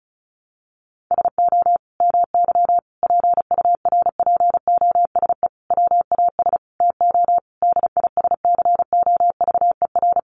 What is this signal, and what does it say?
Morse code, heard as a keyed tone, about 34 wpm, 700 Hz. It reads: SO MY PURPOSE WAS TO DISCOVER
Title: Sentences from Top 1100 Words 35wpm